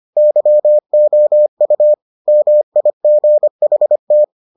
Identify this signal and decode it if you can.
Morse code, heard as a keyed tone, about 25 wpm, 600 Hz. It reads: YOU MIGHT